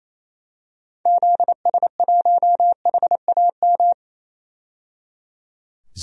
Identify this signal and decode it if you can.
Morse code, heard as a keyed tone, about 28 wpm, 700 Hz. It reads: ZS1HAM
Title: DXCC-100 Call Signs - 6 Characters - 28wpm